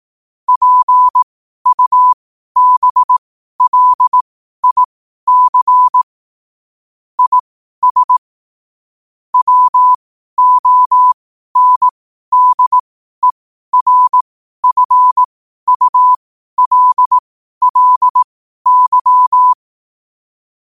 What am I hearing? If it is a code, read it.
Morse code, heard as a keyed tone, about 18 wpm, spaced as for 12 wpm, 1000 Hz. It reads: PUBLIC IS WONDERFULLY